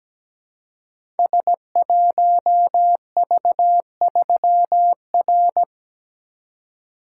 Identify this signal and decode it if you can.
Morse code, heard as a keyed tone, about 17 wpm, 700 Hz. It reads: S1V3R
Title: Code Groups - 5 Letter:Numbers - 17wpm